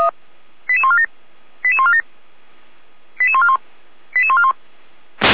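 Five-tone selective call (ZVEI), heard as quick runs of five tones, tones of about 70 ms. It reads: 88147 88147 88141 88141
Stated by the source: ZVEI five-tone call received on an SDR.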